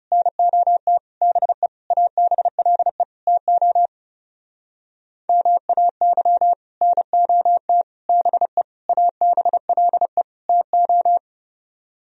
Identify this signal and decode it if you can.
Morse code, heard as a keyed tone, about 37 wpm, 700 Hz. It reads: NOT BE ABLE TO MAY NOT BE ABLE TO
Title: Speed-Racing - Sets of 5 Words 30wpm